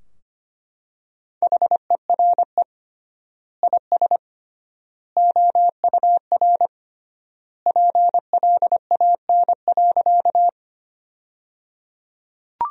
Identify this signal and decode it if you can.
Morse code, heard as a keyed tone, about 25 wpm, 700 Hz. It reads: HERE IS OUR PLAN.
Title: 3x Word-Spacing - Sentences from Top 500 Words - 25wpm